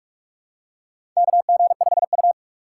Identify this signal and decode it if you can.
Morse code, heard as a keyed tone, about 46 wpm, 700 Hz. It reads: KG5U
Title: USA Call Signs - 6x Character Spacing - 45wpm